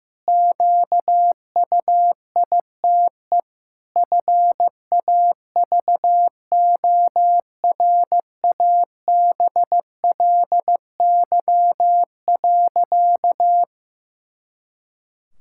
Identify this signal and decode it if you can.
Morse code, heard as a keyed tone, about 15 wpm, 700 Hz. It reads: QUITE FAVORABLY.
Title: Pangram Challenge v2 - 15wpm